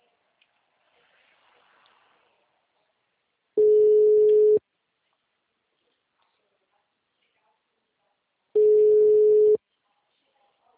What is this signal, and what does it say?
Ringback tone (Europe, 425 Hz), with speech.